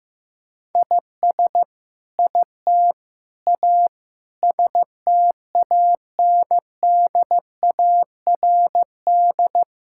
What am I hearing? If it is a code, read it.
Morse code, heard as a keyed tone, about 15 wpm, 700 Hz. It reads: IS IT A STANDARD
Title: Sentences from Top 1000 Words - Review Part 6 - 15wpm